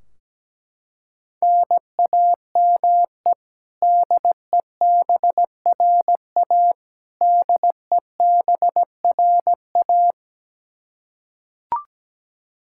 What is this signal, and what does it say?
Morse code, heard as a keyed tone, about 17 wpm, 700 Hz. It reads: NAME DEBRA DEBRA
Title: Practice QSO - Exchange 1 - Beginning - 17wpm